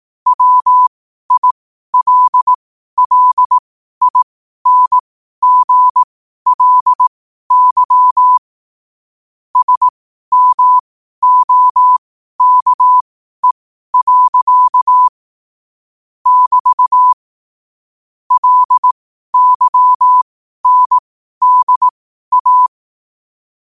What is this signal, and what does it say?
Morse code, heard as a keyed tone, about 18 wpm, spaced as for 12 wpm, 1000 Hz. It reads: WILLINGLY SMOKE. <BT> LYNDA
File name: qotdfiles / 12wpm